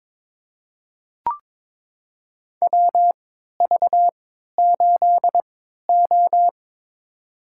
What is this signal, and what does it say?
Morse code, heard as a keyed tone, about 22 wpm, spaced as for 12 wpm, 700 Hz. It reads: WV8O
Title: USA Call Signs - 3x Character Spacing - 22wpm